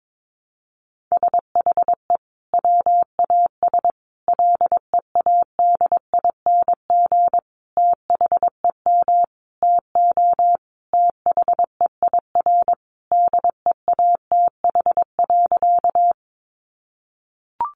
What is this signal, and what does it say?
Morse code, heard as a keyed tone, about 22 wpm, 700 Hz. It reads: SHE WAS LEADING THEM TO THEIR DEATH.
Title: Sentences from Top 1100 Words 22wpm